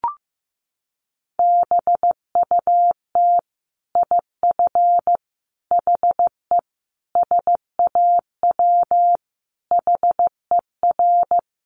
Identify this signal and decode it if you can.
Morse code, heard as a keyed tone, about 15 wpm, 700 Hz. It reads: BUT IF HE SAW HER
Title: Sentences from Top 1400 Words 15wpm